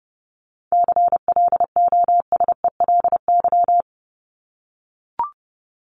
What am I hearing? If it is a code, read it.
Morse code, heard as a keyed tone, about 30 wpm, 700 Hz. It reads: CLOSELY
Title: Top 1251-1500 Words 30wpm